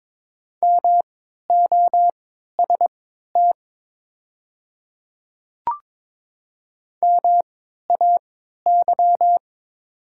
Morse code, heard as a keyed tone, about 22 wpm, spaced as for 12 wpm, 700 Hz. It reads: MOST MAY
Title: 3x Character Spacing - Top 100 Words - 22wpm